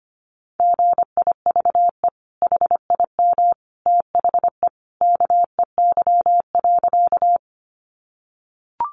Morse code, heard as a keyed tone, about 25 wpm, 700 Hz. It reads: GIVE HIM THE KEY.
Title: Sentences from Top 600 Words 25wpm